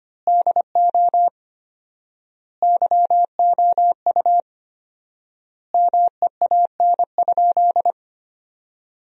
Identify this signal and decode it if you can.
Morse code, heard as a keyed tone, about 25 wpm, 700 Hz. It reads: DO YOU MEAN?
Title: 4x Word-Spacing - Sentences from Top 200 Words - 25wpm